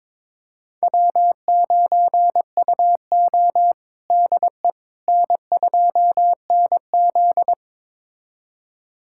Morse code, heard as a keyed tone, about 22 wpm, 700 Hz. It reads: W9UO DE N2NZ
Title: USA Callsign DE Callsign - 4 Characters - 22wpm